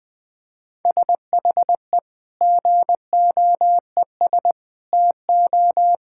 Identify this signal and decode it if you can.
Morse code, heard as a keyed tone, about 20 wpm, 700 Hz. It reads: SHE GOES TO
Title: Sentences with 2-Word Phrases - 20wpm